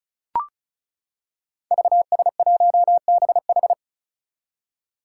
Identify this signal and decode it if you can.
Morse code, heard as a keyed tone, about 36 wpm, 700 Hz. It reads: VS1BH
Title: Five Challenging Letter-Number Combinations 35wpm